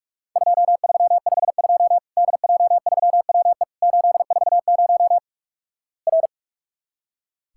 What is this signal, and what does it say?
Morse code, heard as a keyed tone, about 43 wpm, 700 Hz. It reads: 2352 DJ3WE 840 R
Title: Worked All Europe DX Contest - QTC Exchange - 45wpm